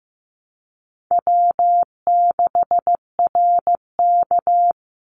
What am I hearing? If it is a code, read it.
Morse code, heard as a keyed tone, about 15 wpm, 700 Hz. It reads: W6RK
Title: Easy Call Signs - 15wpm